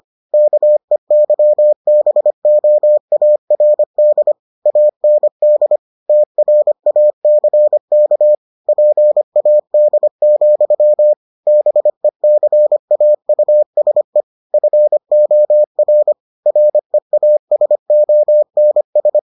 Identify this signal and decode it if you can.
Morse code, heard as a keyed tone, about 25 wpm, 600 Hz. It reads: KEYBOARD AND TRACK PAD, BECAUSE FOR REASONS